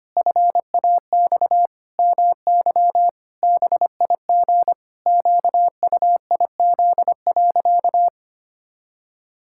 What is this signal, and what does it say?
Morse code, heard as a keyed tone, about 25 wpm, 700 Hz. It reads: FAX MY BIG QUIZ.
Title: Pangram Challenge v2 - 25wpm